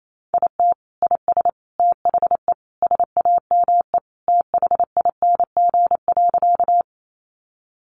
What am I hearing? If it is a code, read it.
Morse code, heard as a keyed tone, about 28 wpm, 700 Hz. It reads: IT IS THE SAME THING.
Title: Sentences from Top 200 Words 28wpm